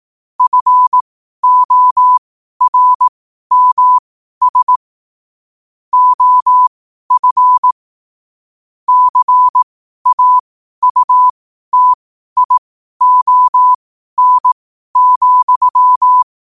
Morse code, heard as a keyed tone, about 18 wpm, spaced as for 12 wpm, 1000 Hz. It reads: FORMS OF CAUTION,
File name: qotdfiles / 12wpm